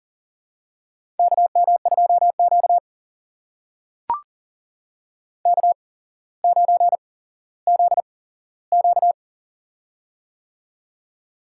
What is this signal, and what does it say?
Morse code, heard as a keyed tone, about 40 wpm, spaced as for 17 wpm, 700 Hz. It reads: KM2Q K9ZQ
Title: USA Call Signs - 4 Characters - 8x Character Spacing - 40wpm